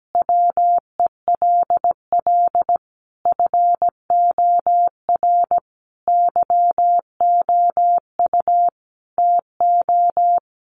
Morse code, heard as a keyed tone, about 17 wpm, 700 Hz. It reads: WELL FOR YOU TO
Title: Sentences from Top 1400 Words 17wpm